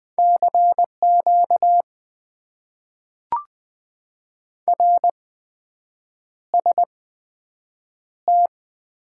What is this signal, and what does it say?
Morse code, heard as a keyed tone, about 20 wpm, spaced as for 7 wpm, 700 Hz. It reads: CQ RST
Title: Top 90 QSO Elements - 8x Character Spacing - 20wpm